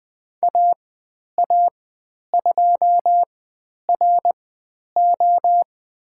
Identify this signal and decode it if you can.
Morse code, heard as a keyed tone, about 20 wpm, spaced as for 10 wpm, 700 Hz. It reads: AA2RO
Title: 095 Lesson 16g - Letter c - Call Signs 20wpm